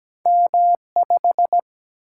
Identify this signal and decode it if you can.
Morse code, heard as a keyed tone, about 17 wpm, 700 Hz. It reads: M5
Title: Single-Letter Single-Number 17wpm